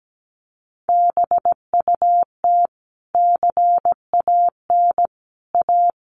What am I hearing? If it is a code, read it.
Morse code, heard as a keyed tone, about 17 wpm, 700 Hz. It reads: BUT CAN A
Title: Sentences from Top 1400 Words 17wpm